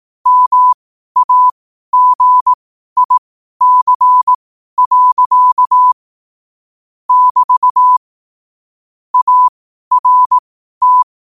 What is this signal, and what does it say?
Morse code, heard as a keyed tone, about 18 wpm, spaced as for 12 wpm, 1000 Hz. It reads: MAGIC. <BT> ART
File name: qotdfiles / 12wpm